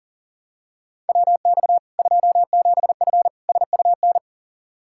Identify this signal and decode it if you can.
Morse code, heard as a keyed tone, about 37 wpm, 700 Hz. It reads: WX 27F SUN
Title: Practice QSO - Exchange 2 - Info - 40wpm